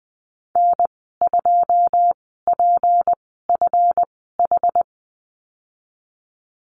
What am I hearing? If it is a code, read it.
Morse code, heard as a keyed tone, about 20 wpm, spaced as for 15 wpm, 700 Hz. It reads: N2PFH
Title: 193 Lesson 31g - Letter v - Call Signs 20wpm